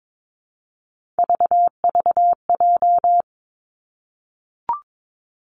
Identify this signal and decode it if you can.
Morse code, heard as a keyed tone, about 22 wpm, 700 Hz. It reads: VVJ
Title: Three Challenging Letter-Number Combinations 22wpm